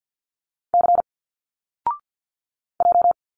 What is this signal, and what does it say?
Morse code, heard as a keyed tone, about 50 wpm, 700 Hz. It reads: C J
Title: Single Letters 50wpm